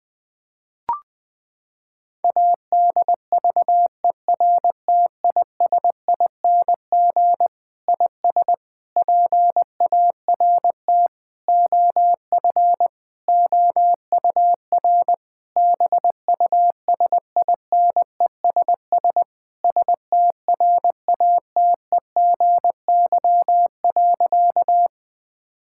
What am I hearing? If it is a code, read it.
Morse code, heard as a keyed tone, about 21 wpm, 700 Hz. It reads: ADVERTISING IS PART OF OUR BUSINESS STRATEGY.
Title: Sentences from Top 1000 Words - Review Part 6 - 20wpm